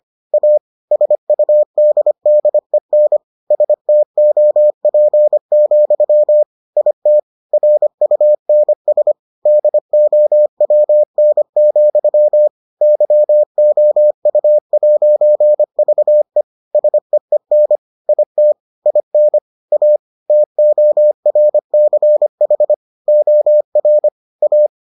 Morse code, heard as a keyed tone, about 25 wpm, 600 Hz. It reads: A SUDDEN STOP, IT RUNS DOWN, YOU'VE SEEN IT IN A TORCH OR A